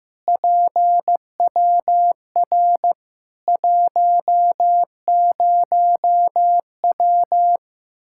Morse code, heard as a keyed tone, about 15 wpm, 700 Hz. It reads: PWR 10W
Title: Practice QSO - Exchange 2 - Info - 15wpm